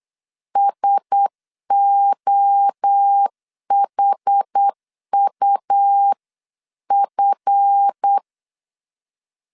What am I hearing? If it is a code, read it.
Morse code, heard as a keyed tone, about 8 wpm, 800 Hz. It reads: SOHUF
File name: CW / DET / 1 9 1500C S C 8wpm 5653